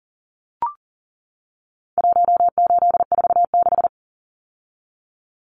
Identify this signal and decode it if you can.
Morse code, heard as a keyed tone, about 40 wpm, 700 Hz. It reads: 1846
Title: Numbers - Contest Style - 4-Digit 40wpm